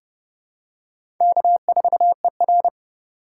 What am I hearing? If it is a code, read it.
Morse code, heard as a keyed tone, about 30 wpm, 700 Hz. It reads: K4ER
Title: USA Call Signs - 4 Characters - 3x Character Spacing - 30wpm